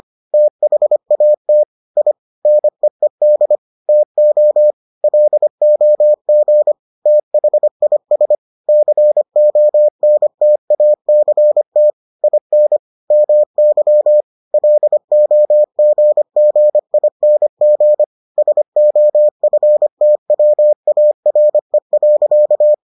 Morse code, heard as a keyed tone, about 25 wpm, 600 Hz. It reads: THAT I NEED TO LOG THIS CONTACT IN MY LOGGING SOFTWARE.